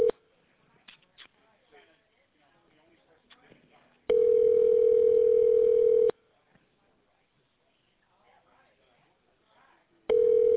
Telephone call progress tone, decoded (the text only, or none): ringback tone